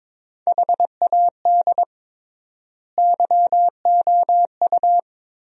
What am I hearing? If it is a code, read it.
Morse code, heard as a keyed tone, about 22 wpm, 700 Hz. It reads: HAD YOU
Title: 3x Word-Spacing - Sentences from Top 400 Words - 22wpm